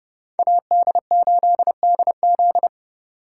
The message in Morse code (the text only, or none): AD8DZ